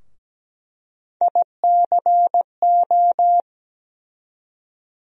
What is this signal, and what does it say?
Morse code, heard as a keyed tone, about 17 wpm, 700 Hz. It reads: ICO
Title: USA Call Sign Postfixes - 3 Letters 17wpm